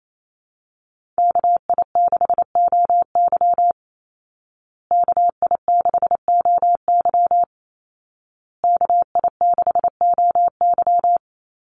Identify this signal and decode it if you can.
Morse code, heard as a keyed tone, about 28 wpm, 700 Hz. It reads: KI6OY KI6OY KI6OY
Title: Call Signs Repeated 2x - 28wpm